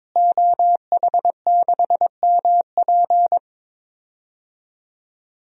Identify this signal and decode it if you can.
Morse code, heard as a keyed tone, about 22 wpm, 700 Hz. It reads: OH6MP
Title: DXCC-100 Call Signs - 5 Characters - Repeated 3x - 22wpm